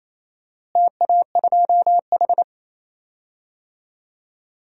Morse code, heard as a keyed tone, about 27 wpm, 700 Hz. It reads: TA2H
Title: DXCC-100 Call Signs - 4 Characters - Repeated 2x - 28wpm